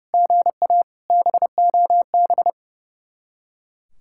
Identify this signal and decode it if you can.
Morse code, heard as a keyed tone, about 36 wpm, 700 Hz. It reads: GA BOB
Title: Speed-Racing Common QSO Phrases - 30wpm